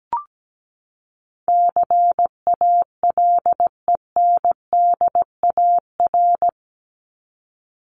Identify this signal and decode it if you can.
Morse code, heard as a keyed tone, about 17 wpm, 700 Hz. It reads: CALENDAR
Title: Top 1501-1750 Words 17wpm